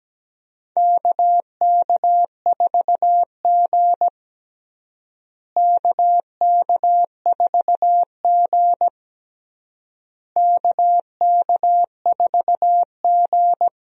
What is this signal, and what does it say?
Morse code, heard as a keyed tone, about 17 wpm, 700 Hz. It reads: KK4G KK4G KK4G
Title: USA Call Signs - 4 Characters - Repeated 3x - 17wpm